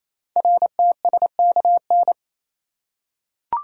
Morse code, heard as a keyed tone, about 29 wpm, 700 Hz. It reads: RTSKN
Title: Code Groups - 5 Letters - 28wpm